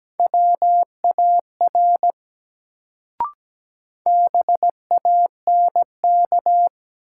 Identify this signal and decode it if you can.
Morse code, heard as a keyed tone, about 17 wpm, 700 Hz. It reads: WAR BANK
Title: Sets of 5 Words - New Word Drill - 17wpm